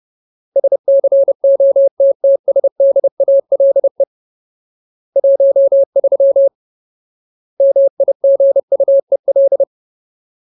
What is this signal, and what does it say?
Morse code, heard as a keyed tone, about 30 wpm, 550 Hz. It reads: SCOTTSDALE 13 MIGUEL
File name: Ultimate Word List / 30WPM Ultimate